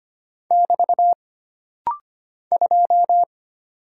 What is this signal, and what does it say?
Morse code, heard as a keyed tone, about 25 wpm, 700 Hz. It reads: <BT> 2
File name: Single Character (Letter:Number:Punctuation:Prosign) - Rapid-Fire - 25wpm